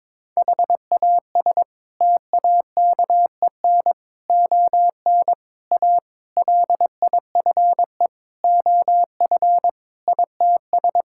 Morse code, heard as a keyed tone, about 22 wpm, 700 Hz. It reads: HAS TAKEN ON A LIFE OF ITS
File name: practice / 22wpm / Sets of 8 Words - 22wpm